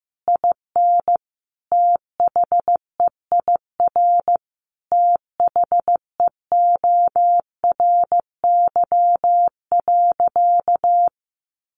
Morse code, heard as a keyed tone, about 15 wpm, 700 Hz. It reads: IN THEIR THEORY.
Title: Sentences from Top 1400 Words 15wpm